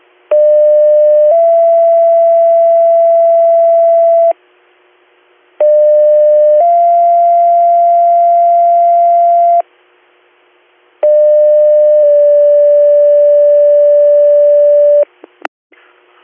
Two-tone page, heard as a one-second tone, then a three-second tone, about 600 then 670 Hz, about 600 then 690 Hz, about 600 then 580 Hz. It is heard from 0.3 to 15.0 s.